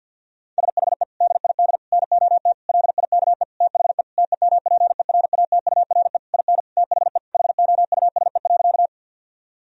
Morse code, heard as a keyed tone, about 50 wpm, 700 Hz. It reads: SHE DID NOT LIKE THE TEMPERATURE IN THE HOUSE.
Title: Sentences from Top 1000 Words - Review Part 4 - 50wpm